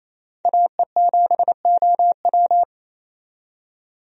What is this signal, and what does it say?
Morse code, heard as a keyed tone, about 28 wpm, 700 Hz. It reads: AE7OW
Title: USA Call Signs - 5 Characters - 28wpm